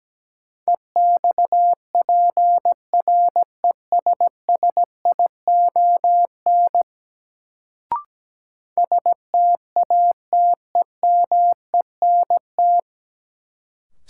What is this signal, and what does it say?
Morse code, heard as a keyed tone, about 17 wpm, 700 Hz. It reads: EXPRESSION STATEMENT
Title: Sentences from Top 1200 Words - New Word Drill Alt - 17wpm